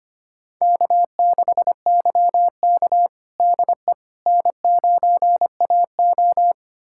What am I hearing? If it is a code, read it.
Morse code, heard as a keyed tone, about 25 wpm, 700 Hz. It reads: K6YK DE N9AO
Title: USA Callsign DE Callsign - 4 Characters - 25wpm